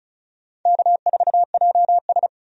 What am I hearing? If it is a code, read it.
Morse code, heard as a keyed tone, about 34 wpm, 700 Hz. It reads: K4JS
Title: USA Call Signs - 4 Characters - 35wpm